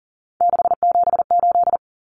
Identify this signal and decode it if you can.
Morse code, heard as a keyed tone, about 40 wpm, 700 Hz. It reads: 678
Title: Common Cut Numbers - 3-Digit 40wpm